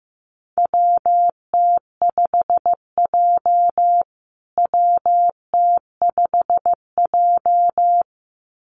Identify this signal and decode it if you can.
Morse code, heard as a keyed tone, about 15 wpm, 700 Hz. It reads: WT5J WT5J
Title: USA Call Signs Repeated 2x 15wpm